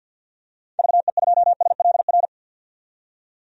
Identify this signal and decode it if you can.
Morse code, heard as a keyed tone, about 52 wpm, 700 Hz. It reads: VE2SLR
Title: DXCC-100 Call Signs - 6 Characters - 50wpm